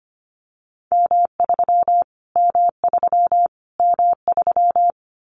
Morse code, heard as a keyed tone, about 25 wpm, 700 Hz. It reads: M3 M3 M3
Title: Callsign Prefixes for Top 25 DXCC - Repeated 3x - 25wpm